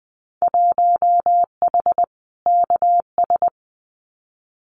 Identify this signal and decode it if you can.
Morse code, heard as a keyed tone, about 20 wpm, 700 Hz. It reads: 1H KS
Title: Winter Field Day Exchanges 20wpm